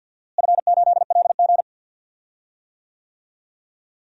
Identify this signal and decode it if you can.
Morse code, heard as a keyed tone, about 52 wpm, 700 Hz. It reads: F8LG
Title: DXCC-100 Call Signs - 4 Characters - Repeated 2x - 50wpm